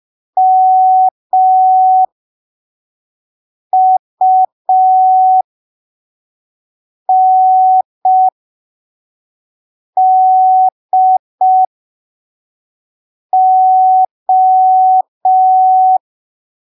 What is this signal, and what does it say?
Morse code, heard as a keyed tone, about 5 wpm, spaced as for 3 wpm, 750 Hz. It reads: MUNDO
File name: arquivos / texto / texto 24 05ppm